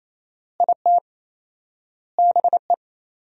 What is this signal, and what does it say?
Morse code, heard as a keyed tone, about 28 wpm, 700 Hz. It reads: IT BE
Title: 4x Word-Spacing - Sentences from Top 500 Words - 28wpm